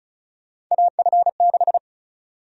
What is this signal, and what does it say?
Morse code, heard as a keyed tone, about 34 wpm, 700 Hz. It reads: AF6
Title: USA Call Sign Prefixes - 35wpm